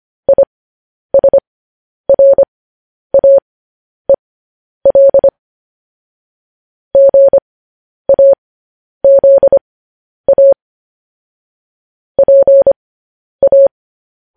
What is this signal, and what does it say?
Morse code, heard as a keyed tone, about 25 wpm, spaced as for 10 wpm, 550 Hz. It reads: ISRAEL GAZA PA